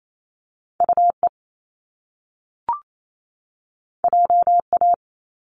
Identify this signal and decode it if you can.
Morse code, heard as a keyed tone, about 28 wpm, 700 Hz. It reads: UE JA
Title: Top Two Letter Combinations in Words 28wpm